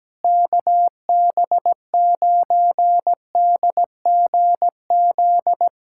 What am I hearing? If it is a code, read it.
Morse code, heard as a keyed tone, about 17 wpm, 700 Hz. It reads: KB9DGZ
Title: USA Callsign DE Callsign - 6 Characters - 17wpm